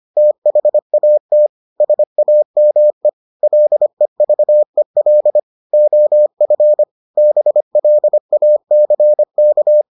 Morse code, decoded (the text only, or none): THAT SAME LEVEL OF BLACK